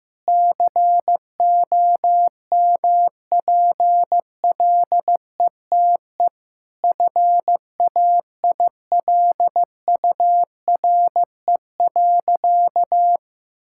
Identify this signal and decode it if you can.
Morse code, heard as a keyed tone, about 15 wpm, 700 Hz. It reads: COMPLETE FAILURE.
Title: Sentences from Top 1000 Words - Review Part 6 - 15wpm